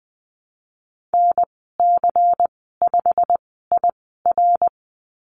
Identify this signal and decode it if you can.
Morse code, heard as a keyed tone, about 20 wpm, spaced as for 15 wpm, 700 Hz. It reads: NC5IR
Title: 161 Lesson 26g - Letter g - Call Signs 20wpm